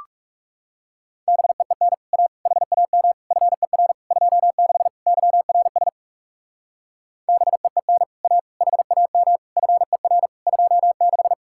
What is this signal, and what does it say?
Morse code, heard as a keyed tone, about 36 wpm, 700 Hz. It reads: BEEN A HAM FER 26 YRS BEEN A HAM FER 26
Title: Speed-Racing Common QSO Phrases - 30wpm